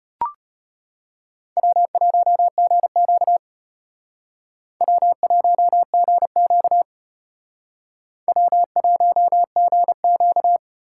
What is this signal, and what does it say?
Morse code, heard as a keyed tone, about 31 wpm, 700 Hz. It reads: W1GQ W1GQ W1GQ